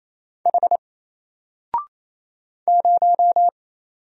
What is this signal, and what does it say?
Morse code, heard as a keyed tone, about 27 wpm, 700 Hz. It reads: H 0
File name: Single-Letter-Number-28wpm